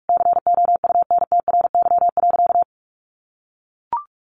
Morse code, heard as a keyed tone, about 44 wpm, 700 Hz. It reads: COUNTRY.
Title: Sentences from Top 1100 Words 45wpm